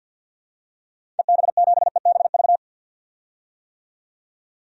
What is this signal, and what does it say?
Morse code, heard as a keyed tone, about 53 wpm, 700 Hz. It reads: EB7EBV
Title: DXCC-100 Call Signs - 6 Characters - 50wpm